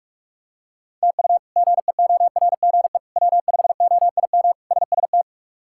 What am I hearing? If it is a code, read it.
Morse code, heard as a keyed tone, about 47 wpm, 700 Hz. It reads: TU GEORGE W5OIM SST
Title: K1USN SST QSO Practice - 45wpm